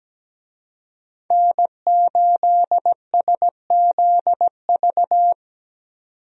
Morse code, heard as a keyed tone, about 17 wpm, 700 Hz. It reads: N8SZV
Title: USA Call Signs - 5 Characters - 17wpm